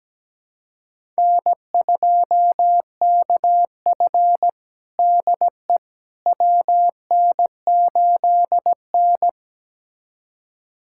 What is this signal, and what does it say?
Morse code, heard as a keyed tone, about 17 wpm, 700 Hz. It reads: N2KF DE WN8N
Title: USA Callsign DE Callsign - 4 Characters - 17wpm